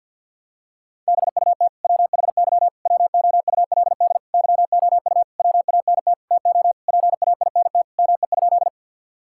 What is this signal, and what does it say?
Morse code, heard as a keyed tone, about 47 wpm, 700 Hz. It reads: BUT WHY WOULD YOU WANT TO PAINT ME?
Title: Sentences from Top 1000 Words - Review Part 6 - 50wpm